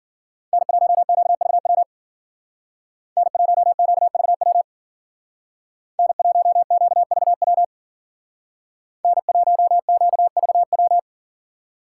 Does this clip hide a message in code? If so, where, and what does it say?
Yes: Morse code from 0.5 to 11.0 s, heard as a keyed tone, about 52 wpm, 700 Hz. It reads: N1QVW N1QVW N1QVW N1QVW